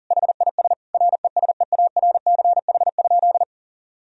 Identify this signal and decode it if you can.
Morse code, heard as a keyed tone, about 40 wpm, 700 Hz. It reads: HIS RESEARCH?
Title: Sentences from Top 1000 Words - Review Part 4 - 40wpm